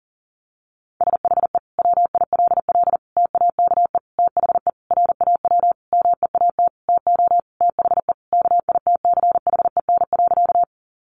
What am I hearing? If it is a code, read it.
Morse code, heard as a keyed tone, about 40 wpm, 700 Hz. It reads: SHE WILL TAKE THE RAW MEAT TO THE KITCHEN.
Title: Sentences from Top 1500 Words 40wpm